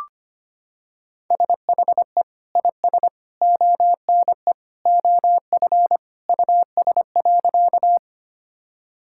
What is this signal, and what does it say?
Morse code, heard as a keyed tone, about 25 wpm, 700 Hz. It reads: SHE IS ONE OF US.
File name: practice / 25wpm / Sentences with 3-Word Phrases - 25wpm